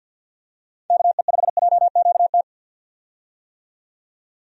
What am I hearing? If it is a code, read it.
Morse code, heard as a keyed tone, about 50 wpm, 700 Hz. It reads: KE5JQT
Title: USA Call Signs - 6 Characters - 50wpm